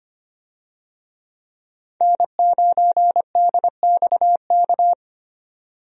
Morse code, heard as a keyed tone, about 25 wpm, 700 Hz. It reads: N9DXK
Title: USA Call Signs - 5 Characters - 25wpm